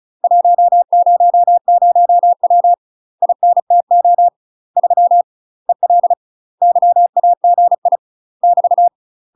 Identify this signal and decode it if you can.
Morse code, heard as a keyed tone, about 35 wpm, 700 Hz. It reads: 100W INTO 3 EL YAGI <BT>
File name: UK MASTER QSOs TRAINING/35WPM